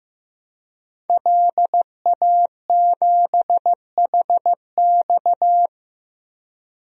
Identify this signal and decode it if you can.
Morse code, heard as a keyed tone, about 15 wpm, 700 Hz. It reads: LA7HX
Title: DXCC-100 Call Signs - 5 Characters - 15wpm